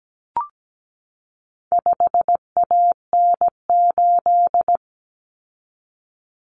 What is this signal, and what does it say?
Morse code, heard as a keyed tone, about 17 wpm, 700 Hz. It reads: 5AN8
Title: Common Cut Numbers - 4-Digit 17wpm